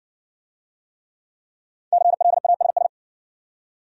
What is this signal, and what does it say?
Morse code, heard as a keyed tone, about 60 wpm, 700 Hz. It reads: CLASS